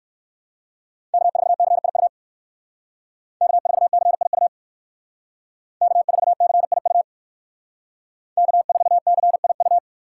Morse code, heard as a keyed tone, about 56 wpm, 700 Hz. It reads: K4CIU K4CIU K4CIU K4CIU
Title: Speed Racing - USA 1x3 Call Signs 45wpm